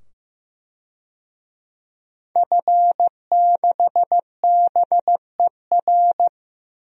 Morse code, heard as a keyed tone, about 15 wpm, 700 Hz. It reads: F6BER
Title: DXCC-100 Call Signs - 5 Characters - 15wpm